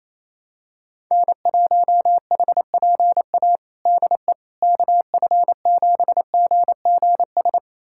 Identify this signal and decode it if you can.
Morse code, heard as a keyed tone, about 28 wpm, 700 Hz. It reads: N1HPA DE KF7GGS
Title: USA Callsign DE Callsign 28wpm